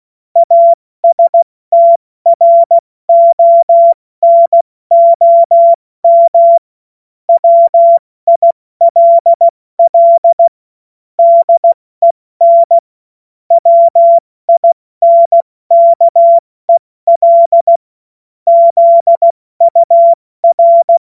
Morse code, heard as a keyed tone, about 16 wpm, 680 Hz. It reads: ASTRONOM WILL DEN WINKEL ZUR